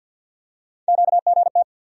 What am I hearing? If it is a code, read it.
Morse code, heard as a keyed tone, about 46 wpm, 700 Hz. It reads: QGT
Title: USA Call Sign Postfixes - 3 Letters 50wpm